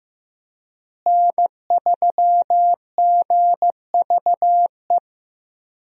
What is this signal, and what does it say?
Morse code, heard as a keyed tone, about 15 wpm, 700 Hz. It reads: N3GVE
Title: USA Call Signs - 5 Characters - Repeated 3x - 15wpm